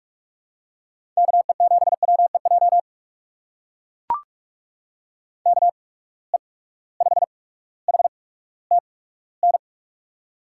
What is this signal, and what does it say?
Morse code, heard as a keyed tone, about 44 wpm, spaced as for 19 wpm, 700 Hz. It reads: KE7WEJ KE5HTN
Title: USA Call Signs - 6 Characters - 8x Character Spacing - 45wpm